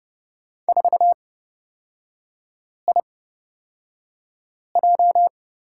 Morse code, heard as a keyed tone, about 30 wpm, spaced as for 5 wpm, 700 Hz. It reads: 4IJ
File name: Lesson 36e - BT - Three Characters 30wpm